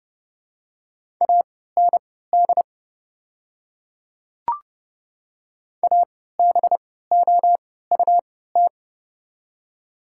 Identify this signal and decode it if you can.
Morse code, heard as a keyed tone, about 30 wpm, spaced as for 17 wpm, 700 Hz. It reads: AND ABOUT
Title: 3x Character Spacing - Top 500 Words - 30wpm